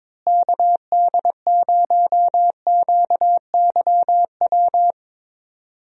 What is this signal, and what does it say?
Morse code, heard as a keyed tone, about 22 wpm, 700 Hz. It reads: KD0QYW
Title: USA Call Signs - 6 Characters - Repeated 2x - 22wpm